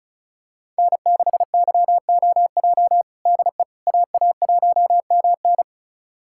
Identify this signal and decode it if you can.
Morse code, heard as a keyed tone, about 36 wpm, 700 Hz. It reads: N6YOJ DE AA1MN
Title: USA Callsign DE Callsign - 5 Characters - 35wpm